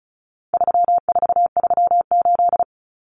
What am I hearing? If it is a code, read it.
Morse code, heard as a keyed tone, about 34 wpm, 700 Hz. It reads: 3438
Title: Numbers - 4-Digit 35wpm